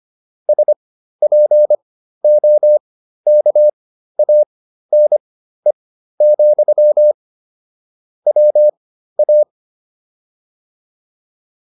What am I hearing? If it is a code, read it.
Morse code, heard as a keyed tone, about 25 wpm, spaced as for 13 wpm, 600 Hz. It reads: SPOKANE, WA
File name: QSO310 13